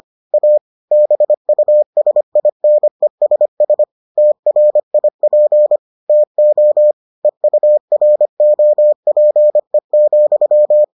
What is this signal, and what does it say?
Morse code, heard as a keyed tone, about 25 wpm, 600 Hz. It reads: A BUSINESS TRIP TO EUROPE,